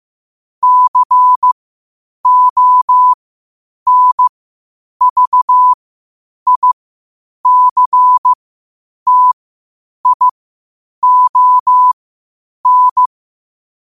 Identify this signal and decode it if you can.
Morse code, heard as a keyed tone, about 15 wpm, spaced as for 8 wpm, 1000 Hz. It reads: CONVICTION
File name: qotdfiles / 8wpm